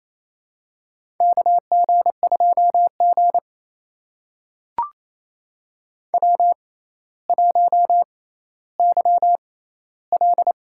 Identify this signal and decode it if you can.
Morse code, heard as a keyed tone, about 28 wpm, spaced as for 14 wpm, 700 Hz. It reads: KG2G W1YL
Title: USA Call Signs - 4 Characters - 6x Character Spacing - 28wpm